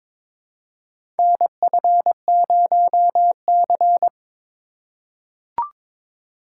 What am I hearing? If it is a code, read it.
Morse code, heard as a keyed tone, about 22 wpm, 700 Hz. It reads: NF0C
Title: USA Call Signs - 4 Characters - 4x Character Spacing - 22wpm